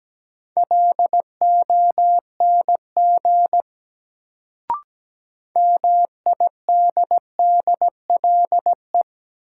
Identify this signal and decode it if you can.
Morse code, heard as a keyed tone, about 17 wpm, 700 Hz. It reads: LONG MIDDLE
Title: Sets of 2 Words - New Word Drill - 17wpm